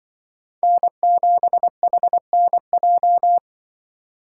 Morse code, heard as a keyed tone, about 25 wpm, 700 Hz. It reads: N7HNJ
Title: Speed Racing - USA 1x3 Call Signs 20wpm